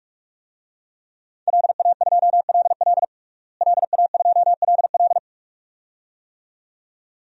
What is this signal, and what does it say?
Morse code, heard as a keyed tone, about 46 wpm, 700 Hz. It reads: LA2LL LA2LL
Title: DXCC-100 Call Signs - 5 Characters - Repeated 2x - 45wpm